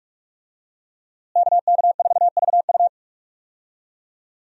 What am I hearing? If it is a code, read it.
Morse code, heard as a keyed tone, about 46 wpm, 700 Hz. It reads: KK4VU
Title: USA Call Signs - 5 Characters - Repeated 2x - 45wpm